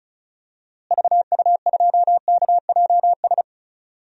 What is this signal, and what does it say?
Morse code, heard as a keyed tone, about 36 wpm, 700 Hz. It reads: VU2KJS
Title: DXCC-100 Call Signs - 6 Characters - Repeated 3x - 35wpm